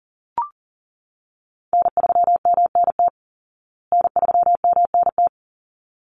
Morse code, heard as a keyed tone, about 37 wpm, 700 Hz. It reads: N3MNT N3MNT
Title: Call Signs Repeated 3x - 40wpm